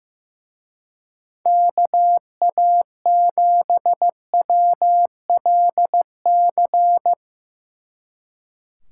Morse code, heard as a keyed tone, about 16 wpm, 700 Hz. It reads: KA7WLC
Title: Speed Racing - USA 2x3 Call Signs 15wpm